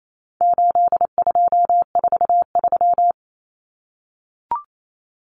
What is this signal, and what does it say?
Morse code, heard as a keyed tone, about 28 wpm, 700 Hz. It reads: 8243
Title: Common Cut Numbers - 4-Digit 28wpm